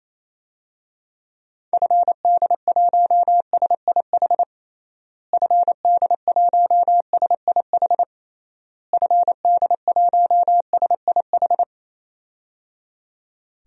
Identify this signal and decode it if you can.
Morse code, heard as a keyed tone, about 27 wpm, 700 Hz. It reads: FD1SIH FD1SIH FD1SIH
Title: DXCC-100 Call Signs - 6 Characters - Repeated 3x - 28wpm